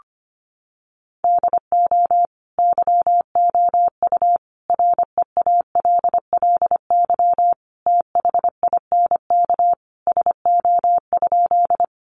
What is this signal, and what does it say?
Morse code, heard as a keyed tone, about 25 wpm, 700 Hz. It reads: DO YOU REALLY THINK SO?
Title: Sentences from Top 600 Words 25wpm